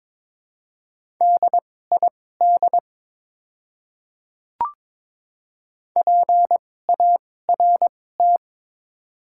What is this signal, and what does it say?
Morse code, heard as a keyed tone, about 22 wpm, spaced as for 16 wpm, 700 Hz. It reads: DID PART
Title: 2x Character Spacing - Top 200 Words - 22wpm